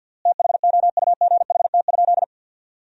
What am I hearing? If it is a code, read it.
Morse code, heard as a keyed tone, about 53 wpm, 700 Hz. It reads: THOUGHT?
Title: 3x Word-Spacing - Sentences from Top 300 Words - 50wpm